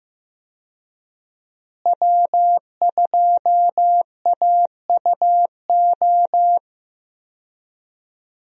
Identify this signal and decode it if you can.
Morse code, heard as a keyed tone, about 15 wpm, 700 Hz. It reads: W2AUO
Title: USA Call Signs - 5 Characters - Repeated 3x - 15wpm